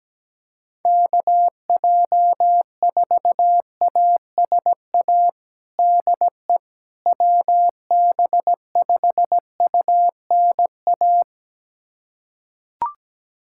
Morse code, heard as a keyed tone, about 17 wpm, 700 Hz. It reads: KJ4ASA DE WB5UNA
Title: USA Callsign DE Callsign - 6 Characters - 17wpm